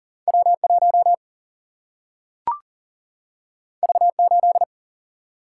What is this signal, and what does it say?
Morse code, heard as a keyed tone, about 40 wpm, 700 Hz. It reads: W1 V8
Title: Single-Letter Single-Number 40wpm